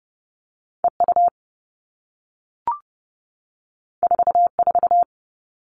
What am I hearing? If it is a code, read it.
Morse code, heard as a keyed tone, about 30 wpm, 700 Hz. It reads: EU 44
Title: All Cut Numbers - 2-Digit 30wpm